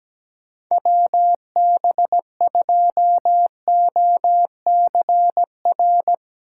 Morse code, heard as a keyed tone, about 17 wpm, 700 Hz. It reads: WB2OCR
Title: Speed Racing - USA 2x3 Call Signs 17wpm